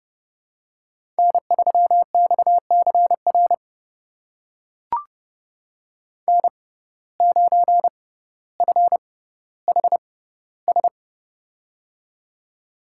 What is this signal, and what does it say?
Morse code, heard as a keyed tone, about 30 wpm, spaced as for 15 wpm, 700 Hz. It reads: N3XCR N9FHS
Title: USA Call Signs - 5 Characters - 6x Character Spacing - 30wpm